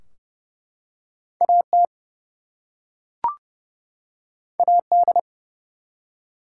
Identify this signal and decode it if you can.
Morse code, heard as a keyed tone, about 30 wpm, 700 Hz. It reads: AT AD
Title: Two Letter Words - 30wpm